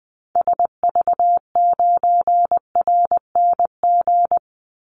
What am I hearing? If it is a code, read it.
Morse code, heard as a keyed tone, about 20 wpm, 700 Hz. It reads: SV9RNG
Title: Call Signs Repeated 3x - 20wpm